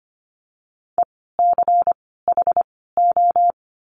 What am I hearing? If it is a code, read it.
Morse code, heard as a keyed tone, about 25 wpm, 700 Hz. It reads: E C H O
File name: course / 25:10wpm / 094 Lesson 16f - Letter c - Words 25wpm